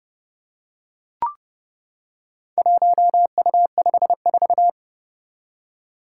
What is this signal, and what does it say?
Morse code, heard as a keyed tone, about 30 wpm, 700 Hz. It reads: 1U54
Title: Four Challenging Letter-Number Combinations 30wpm